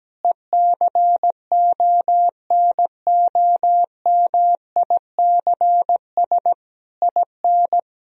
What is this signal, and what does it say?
Morse code, heard as a keyed tone, about 17 wpm, 700 Hz. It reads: ECONOMICS IN
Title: Sentences from Top 1000 Words - Review Part 3 - 17wpm